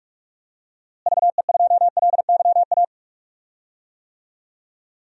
Morse code, heard as a keyed tone, about 43 wpm, 700 Hz. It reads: VE2LYA
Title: DXCC-100 Call Signs - 6 Characters - Repeated 2x - 45wpm